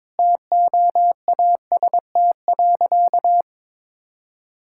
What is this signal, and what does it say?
Morse code, heard as a keyed tone, about 22 wpm, 700 Hz. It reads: TOAST.
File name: practice / 22wpm / Easy Sentences with Hints - 22wpm